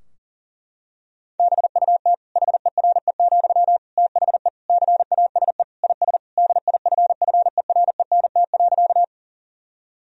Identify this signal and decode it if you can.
Morse code, heard as a keyed tone, about 40 wpm, 700 Hz. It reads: BUT HERE, THE CASE IS DIFFERENT.
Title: Sentences from Top 1000 Words - Review Part 3 - 40wpm